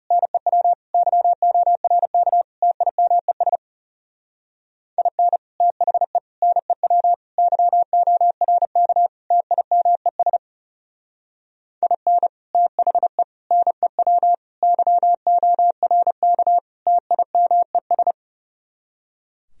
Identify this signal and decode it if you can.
Morse code, heard as a keyed tone, about 38 wpm, 700 Hz. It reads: NEW YORK TIMES IN THE NEW YORK TIMES IN THE NEW YORK TIMES